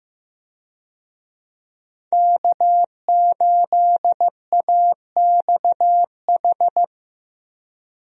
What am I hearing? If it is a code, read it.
Morse code, heard as a keyed tone, about 15 wpm, 700 Hz. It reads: K8AXH
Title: USA Call Signs - 5 Characters - 15wpm